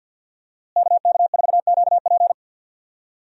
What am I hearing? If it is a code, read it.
Morse code, heard as a keyed tone, about 50 wpm, 700 Hz. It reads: KK4QP
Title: USA Call Signs - 5 Characters - Repeated 3x - 50wpm